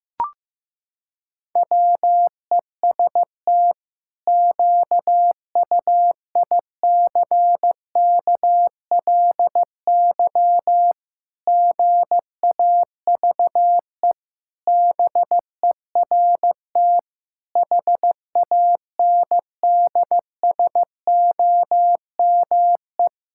Morse code, heard as a keyed tone, about 15 wpm, 700 Hz. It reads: WEST QUICKLY GAVE BERT HANDSOME
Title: Pangram Challenge v2 - 15wpm